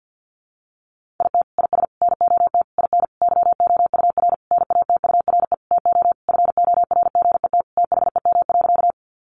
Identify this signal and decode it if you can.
Morse code, heard as a keyed tone, about 50 wpm, 700 Hz. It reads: IT IS NOT IN YOUR NATURE TO FORGET THEM.
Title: Sentences from Top 1200 Words 50wpm